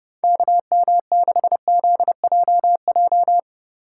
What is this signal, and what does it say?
Morse code, heard as a keyed tone, about 29 wpm, 700 Hz. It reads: KM6ZJJ